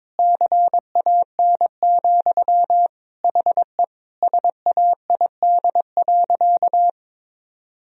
Morse code, heard as a keyed tone, about 22 wpm, 700 Hz. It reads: CAN, HE SAID.
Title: Sentences from Top 1000 Words - Review Part 1 - 22wpm